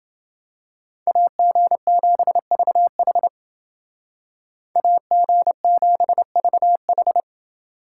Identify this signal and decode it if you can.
Morse code, heard as a keyed tone, about 28 wpm, 700 Hz. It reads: AG7VH AG7VH